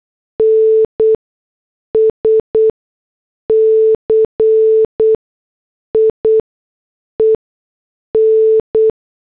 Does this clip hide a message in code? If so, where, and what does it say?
Yes: Morse code from 0.4 to 8.9 s, heard as a keyed tone, about 8 wpm, spaced as for 6 wpm, 440 Hz. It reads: NSCIEN